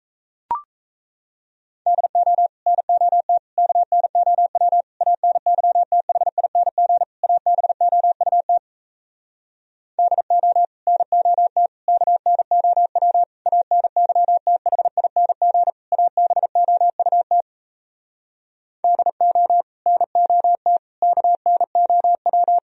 Morse code, heard as a keyed tone, about 37 wpm, 700 Hz. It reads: DO NOT KNOW ANYTHING ABOUT DO NOT KNOW ANYTHING ABOUT DO NOT KNOW